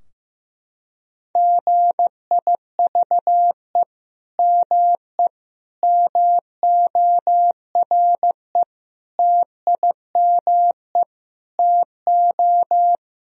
Morse code, heard as a keyed tone, about 15 wpm, 700 Hz. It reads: GIVE ME MORE TIME TO
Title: Sentences from Top 1000 Words - Review Part 3 - 15wpm